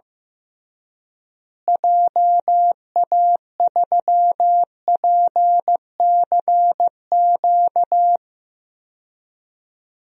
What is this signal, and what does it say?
Morse code, heard as a keyed tone, about 15 wpm, 700 Hz. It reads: JA3PCQ
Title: DXCC-100 Call Signs - 6 Characters - Repeated 3x - 15wpm